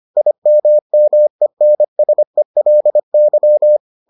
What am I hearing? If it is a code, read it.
Morse code, heard as a keyed tone, about 25 wpm, 600 Hz. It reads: IMMENSELY